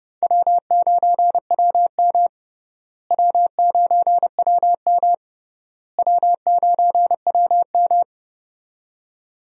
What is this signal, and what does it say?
Morse code, heard as a keyed tone, about 31 wpm, 700 Hz. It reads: W9WM W9WM W9WM
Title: USA Call Signs - 4 Characters - Repeated 3x - 30wpm